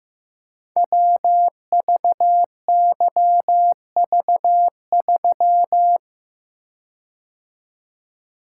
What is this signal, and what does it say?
Morse code, heard as a keyed tone, about 15 wpm, 700 Hz. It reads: WVYV3
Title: Code Groups - 5 Letter:Numbers - 15wpm